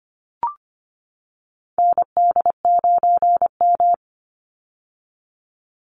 Morse code, heard as a keyed tone, about 25 wpm, 700 Hz. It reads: ND9M
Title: Call Signs - 25wpm